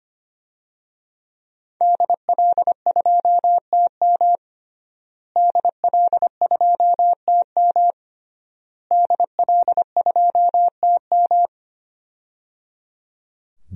Morse code, heard as a keyed tone, about 25 wpm, 700 Hz. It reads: DL2TM DL2TM DL2TM
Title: DXCC-100 Call Signs - 5 Characters - Repeated 3x - 25wpm